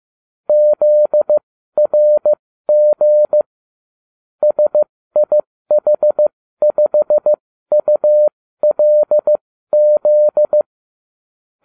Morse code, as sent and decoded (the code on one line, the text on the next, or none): --.. .-. --. / ... .. .... ..... ..- .-.. --..
ZRG SIH5ULZ